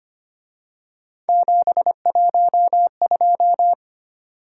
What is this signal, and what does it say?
Morse code, heard as a keyed tone, about 28 wpm, 700 Hz. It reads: 712
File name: Speed-Racing - Three Letter-Number Combinations - 25wpm